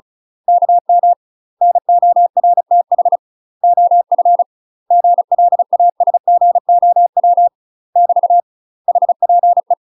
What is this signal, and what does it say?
Morse code, heard as a keyed tone, about 36 wpm, 700 Hz. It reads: KM NORTH OF GLASGOW <BT> HPE
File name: UK MASTER QSOs TRAINING / 35WPM / UK MASTER QSOs 0031